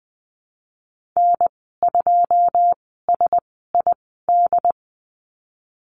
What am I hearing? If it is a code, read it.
Morse code, heard as a keyed tone, about 20 wpm, spaced as for 15 wpm, 700 Hz. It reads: N2SID
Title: 128 Lesson 21g - Number 6 - Call Signs 20wpm